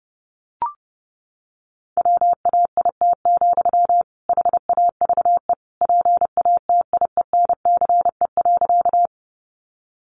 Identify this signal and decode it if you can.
Morse code, heard as a keyed tone, about 30 wpm, 700 Hz. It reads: WAIT, HAVE PATIENCE.
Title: Sentences from Top 1300 Words 30wpm